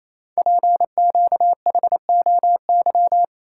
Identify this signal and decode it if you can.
Morse code, heard as a keyed tone, about 27 wpm, 700 Hz. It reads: PQHOY
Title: Code Groups - 5 Letters - 28wpm